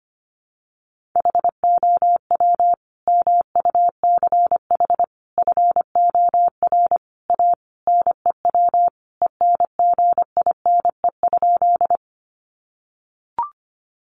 Morse code, heard as a keyed tone, about 25 wpm, 700 Hz. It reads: HOW MUCH FOR A NEW ENGINE?
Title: Sentences from Top 1500 Words 25wpm